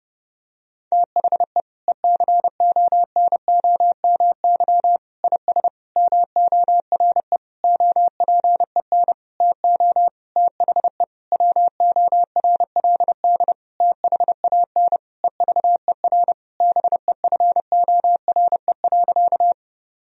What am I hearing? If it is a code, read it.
Morse code, heard as a keyed tone, about 30 wpm, 700 Hz. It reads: THE ECONOMY IS MORE OPEN TO THE WORLD THAN EVER BEFORE.
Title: Sentences from Top 1000 Words - Review Part 3 - 30wpm